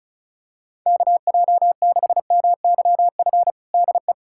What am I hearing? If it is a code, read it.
Morse code, heard as a keyed tone, about 34 wpm, 700 Hz. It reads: KJ6MYF DE
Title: USA Callsign DE Callsign 35wpm